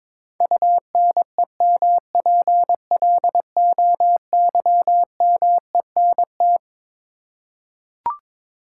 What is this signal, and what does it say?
Morse code, heard as a keyed tone, about 22 wpm, 700 Hz. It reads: UNEMPLOYMENT
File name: Twelve Letter Words - 22wpm